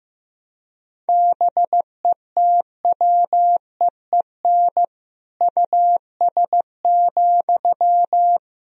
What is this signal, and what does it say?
Morse code, heard as a keyed tone, about 15 wpm, 700 Hz. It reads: BETWEEN US,
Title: Sentences from Top 1000 Words - Review Part 2 - 15wpm